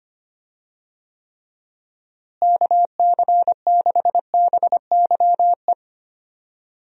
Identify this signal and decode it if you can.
Morse code, heard as a keyed tone, about 25 wpm, 700 Hz. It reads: KC6BYE
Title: USA Call Signs - 6 Characters - Repeated 3x - 25wpm